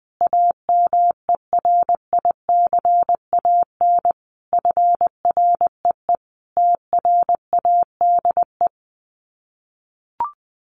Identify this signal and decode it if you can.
Morse code, heard as a keyed tone, about 20 wpm, 700 Hz. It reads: AMERICAN FREE TRADE
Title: Sets of 5 Words Encore 20wpm